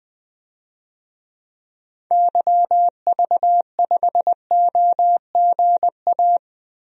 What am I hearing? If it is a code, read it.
Morse code, heard as a keyed tone, about 20 wpm, 700 Hz. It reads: YV5OGA
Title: DXCC-100 Call Signs - 6 Characters - 20wpm